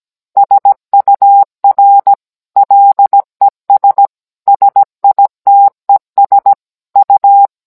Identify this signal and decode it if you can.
Morse code, heard as a keyed tone, about 17 wpm, 800 Hz. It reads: SUR LES SITES U